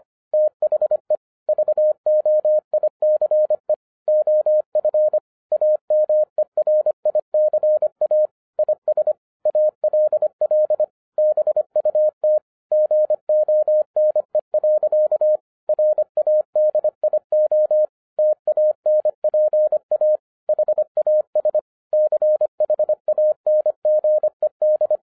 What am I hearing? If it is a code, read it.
Morse code, heard as a keyed tone, about 25 wpm, 610 Hz. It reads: THE VOICE OF AMERICA IS ALL BUT GONE. RADIO TANPA HAS CHANGED